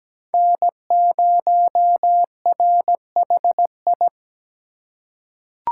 Morse code, heard as a keyed tone, about 17 wpm, 700 Hz. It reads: N0RHI
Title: USA Call Signs - 5 Characters - 3x Character Spacing - 17wpm